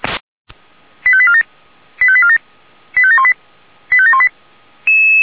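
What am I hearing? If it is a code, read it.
Five-tone selective call (ZVEI), heard as quick runs of five tones, tones of about 70 ms. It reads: 85847 85847 85718 85718
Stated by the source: ZVEI five-tone call received on an SDR.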